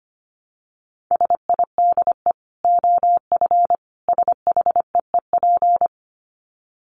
Morse code, heard as a keyed tone, about 25 wpm, 700 Hz. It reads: SIDE OF SHEEP
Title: Dit Challenge 25wpm